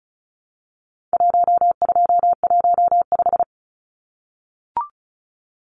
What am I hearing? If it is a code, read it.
Morse code, heard as a keyed tone, about 34 wpm, 700 Hz. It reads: 1215
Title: Numbers - 4-Digit 35wpm